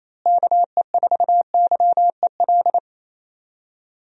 Morse code, heard as a keyed tone, about 28 wpm, 700 Hz. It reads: KE4YEL
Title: USA Call Signs - 6 Characters - Repeated 3x - 28wpm